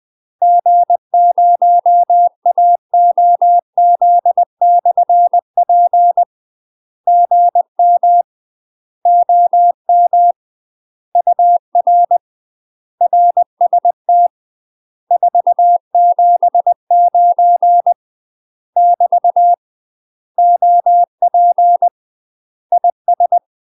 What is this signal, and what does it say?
Morse code, heard as a keyed tone, about 20 wpm, 700 Hz. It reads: G0AOZ/P GM OM UR RST 479 <BT> OP IS
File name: UK MASTER QSOs TRAINING / 20WPM / UK MASTER QSOs 0040